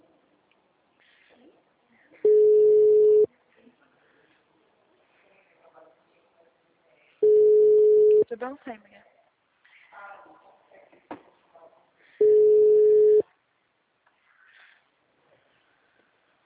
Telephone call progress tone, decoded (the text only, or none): ringback tone